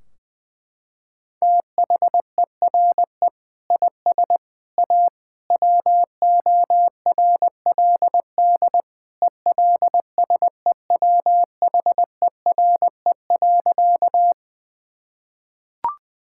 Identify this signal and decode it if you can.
Morse code, heard as a keyed tone, about 20 wpm, 700 Hz. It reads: THERE IS A WORLD ELSEWHERE.
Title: Sentences from Top 1600 Words 20wpm